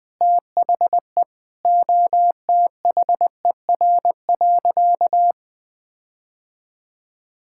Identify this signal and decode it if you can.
Morse code, heard as a keyed tone, about 20 wpm, 700 Hz. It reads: THE OTHER.
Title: Sentences with 5-Word Phrases - 20wpm